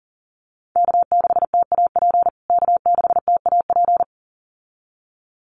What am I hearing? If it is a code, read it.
Morse code, heard as a keyed tone, about 40 wpm, 700 Hz. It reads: K6TAP K6TAP
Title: USA Call Signs Repeated 2x 40wpm